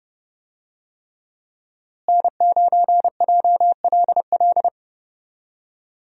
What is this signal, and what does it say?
Morse code, heard as a keyed tone, about 30 wpm, 700 Hz. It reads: N9JLL
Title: USA Call Signs - 5 Characters - Repeated 3x - 30wpm